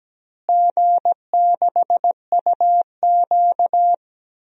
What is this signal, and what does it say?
Morse code, heard as a keyed tone, about 17 wpm, 700 Hz. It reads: G6UQ
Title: DXCC-100 Call Signs - 4 Characters - Repeated 3x - 17wpm